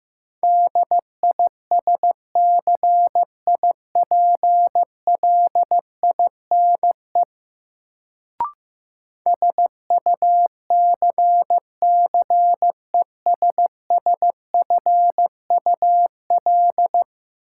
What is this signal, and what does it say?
Morse code, heard as a keyed tone, about 15 wpm, 700 Hz. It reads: DISCIPLINE SUCCESSFUL
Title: Sentences from Top 900 Words - New Word Drill - 15wpm